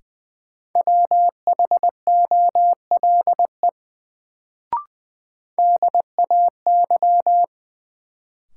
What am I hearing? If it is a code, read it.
Morse code, heard as a keyed tone, about 21 wpm, 700 Hz. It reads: WHOLE DAY
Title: Sets of 2 Words - New Word Drill - 20wpm